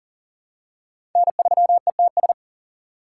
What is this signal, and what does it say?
Morse code, heard as a keyed tone, about 40 wpm, 700 Hz. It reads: N3ETS
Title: USA Call Signs - 5 Characters - 3x Character Spacing - 40wpm